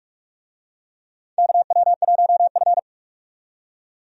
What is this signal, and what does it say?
Morse code, heard as a keyed tone, about 46 wpm, 700 Hz. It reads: KW1F
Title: USA Call Signs - 4 Characters - 45wpm